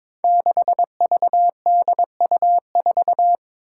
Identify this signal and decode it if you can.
Morse code, heard as a keyed tone, about 22 wpm, 700 Hz. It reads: 6VDU4
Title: Five Challenging Letter-Number Combinations 22wpm